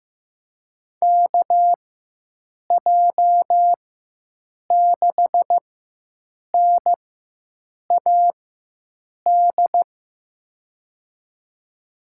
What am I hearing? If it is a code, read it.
Morse code, heard as a keyed tone, about 15 wpm, spaced as for 7 wpm, 700 Hz. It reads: KJ6NAD
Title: USA Call Signs - 6 Characters - 4x Character Spacing - 15wpm